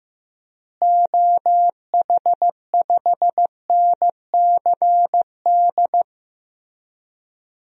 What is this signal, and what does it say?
Morse code, heard as a keyed tone, about 16 wpm, 700 Hz. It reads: OH5NCD
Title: DXCC-100 Call Signs - 6 Characters - Repeated 3x - 15wpm